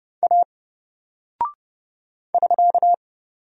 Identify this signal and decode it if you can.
Morse code, heard as a keyed tone, about 31 wpm, 700 Hz. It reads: A <SK>
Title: Single Character (Letter:Number:Punctuation:Prosign) 30wpm